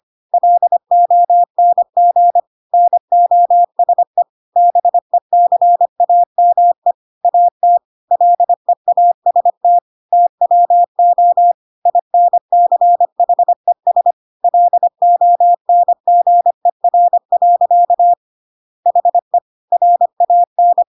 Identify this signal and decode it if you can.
Morse code, heard as a keyed tone, about 25 wpm, 700 Hz. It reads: LONG NOSE BECAME AT LEAST TWO INCHES LONGER. HE RAN
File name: Pinocchio / Pinocchio ch5 25 25 ik1hge